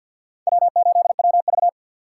Morse code, heard as a keyed tone, about 53 wpm, 700 Hz. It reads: W8WV
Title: USA Call Signs - 4x Character Spacing - 50wpm